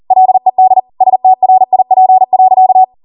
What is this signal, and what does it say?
Morse code, heard as a keyed tone, about 40 wpm, 750 Hz. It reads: LED STRIP.